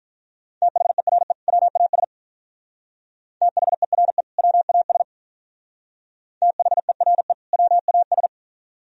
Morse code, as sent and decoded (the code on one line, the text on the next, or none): - .... .-. / .-- .- ... / - .... .-. / .-- .- ... / - .... .-. / .-- .- ...
THERE WAS THERE WAS THERE WAS